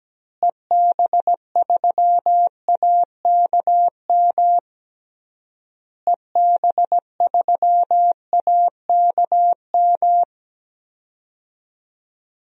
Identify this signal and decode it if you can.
Morse code, heard as a keyed tone, about 17 wpm, 700 Hz. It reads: EB3AKM EB3AKM
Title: DXCC-100 Call Signs - 6 Characters - Repeated 3x - 17wpm